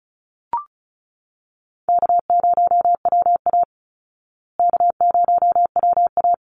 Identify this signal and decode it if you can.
Morse code, heard as a keyed tone, about 34 wpm, 700 Hz. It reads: K0WA K0WA
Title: Easy Call Signs Repeated 2x - 35wpm